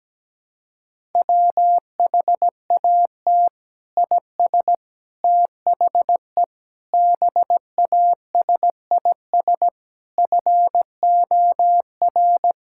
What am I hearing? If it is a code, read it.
Morse code, heard as a keyed tone, about 17 wpm, 700 Hz. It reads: WHAT IS THE BASIS FOR
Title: Sentences from Top 1000 Words - Review Part 6 - 17wpm